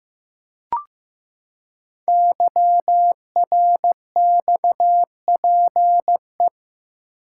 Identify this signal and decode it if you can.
Morse code, heard as a keyed tone, about 15 wpm, 700 Hz. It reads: YRXPE
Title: Code Groups - 5 Letters - 15wpm